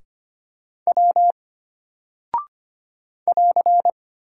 Morse code, heard as a keyed tone, about 25 wpm, 700 Hz. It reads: W <AR>
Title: Single Character (Letter:Number:Punctuation:Prosign) Alt 25wpm